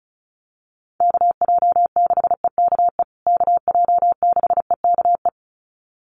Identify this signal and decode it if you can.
Morse code, heard as a keyed tone, about 34 wpm, 700 Hz. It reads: KJ6EKE KJ6EKE
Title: USA Call Signs Repeated 2x 35wpm